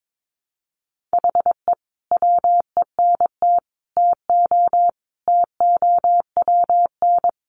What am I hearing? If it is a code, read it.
Morse code, heard as a keyed tone, about 22 wpm, 700 Hz. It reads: HE WENT TO TOWN
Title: Sentences from Top 1300 Words 22wpm